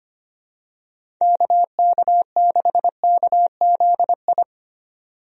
Morse code, heard as a keyed tone, about 25 wpm, 700 Hz. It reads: KK6KZI
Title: USA Call Signs - 6 Characters - 25wpm